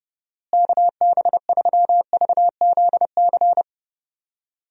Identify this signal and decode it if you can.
Morse code, heard as a keyed tone, about 30 wpm, 700 Hz. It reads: KB3VZC
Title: USA Call Signs - 6 Characters - Repeated 3x - 30wpm